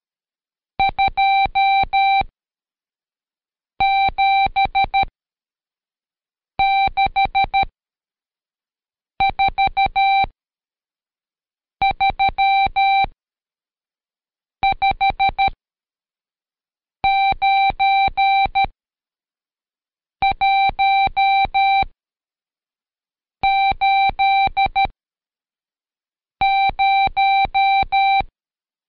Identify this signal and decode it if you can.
Morse code, heard as a keyed tone, about 13 wpm, spaced as for 5 wpm, 780 Hz. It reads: 2764359180
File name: Code Quick / AudioDownld2 / Seg99-Day11-CheckTest 11